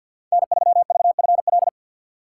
Morse code, heard as a keyed tone, about 52 wpm, 700 Hz. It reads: N3VFL
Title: USA Call Signs - 5 Characters - 6x Character Spacing - 50wpm